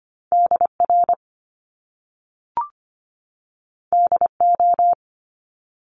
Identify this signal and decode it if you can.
Morse code, heard as a keyed tone, about 25 wpm, 700 Hz. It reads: DR DO
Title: Top Two Letter Combinations in Words 25wpm